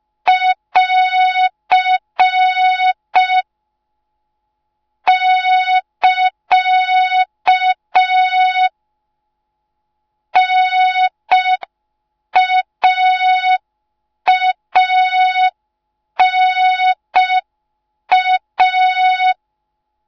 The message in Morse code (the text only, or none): <AR> <KA> NAANA